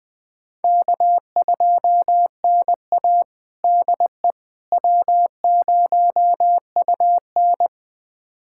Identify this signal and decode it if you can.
Morse code, heard as a keyed tone, about 20 wpm, 700 Hz. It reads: K2NA DE W0UN
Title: USA Callsign DE Callsign - 4 Characters - 20wpm